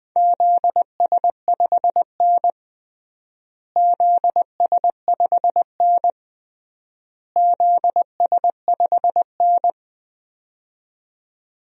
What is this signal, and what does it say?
Morse code, heard as a keyed tone, about 20 wpm, 700 Hz. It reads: ZS5N ZS5N ZS5N
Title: DXCC-100 Call Signs - 4 Characters - Repeated 3x - 20wpm